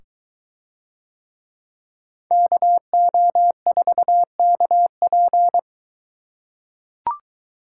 Morse code, heard as a keyed tone, about 21 wpm, 700 Hz. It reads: KO4KP